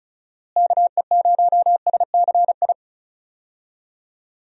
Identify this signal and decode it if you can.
Morse code, heard as a keyed tone, about 34 wpm, 700 Hz. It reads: KE0SCI
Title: USA Call Signs - 6 Characters - Repeated 2x - 35wpm